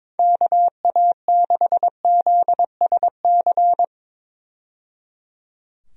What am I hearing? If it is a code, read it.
Morse code, heard as a keyed tone, about 21 wpm, 700 Hz. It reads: KA6ZSC
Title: USA Call Signs - 6 Characters - 22wpm